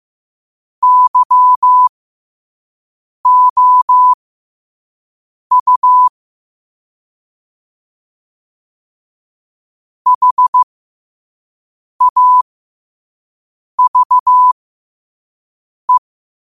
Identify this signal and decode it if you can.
Morse code, heard as a keyed tone, about 15 wpm, spaced as for 5 wpm, 1000 Hz. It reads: YOU HAVE
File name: qotdfiles / 5wpm